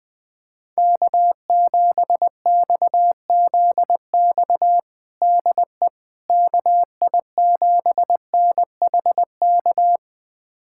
Morse code, heard as a keyed tone, about 21 wpm, 700 Hz. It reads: K7XZX DE KI7NHK
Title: USA Callsign DE Callsign 20wpm